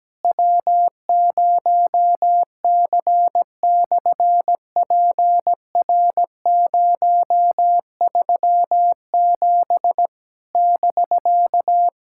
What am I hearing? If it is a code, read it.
Morse code, heard as a keyed tone, about 17 wpm, 700 Hz. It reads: W0C/PR037 <BK>
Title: SOTA QSO Practice 17wpm